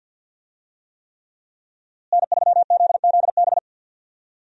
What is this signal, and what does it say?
Morse code, heard as a keyed tone, about 52 wpm, 700 Hz. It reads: N3ZZB
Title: USA Call Signs - 5 Characters - Repeated 3x - 50wpm